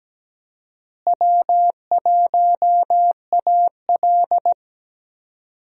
Morse code, heard as a keyed tone, about 17 wpm, 700 Hz. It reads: W1AL
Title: USA Call Signs - 4 Characters - Repeated 2x - 17wpm